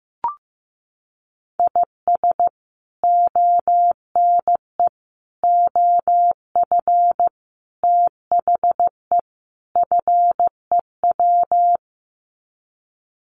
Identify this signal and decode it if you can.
Morse code, heard as a keyed tone, about 15 wpm, 700 Hz. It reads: IS ONE OF THE FEW
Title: Sets of 5 Words 15wpm